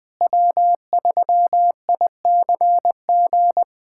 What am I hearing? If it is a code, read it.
Morse code, heard as a keyed tone, about 21 wpm, 700 Hz. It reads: W3ICG
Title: USA Callsign DE Callsign - 5 Characters - 20wpm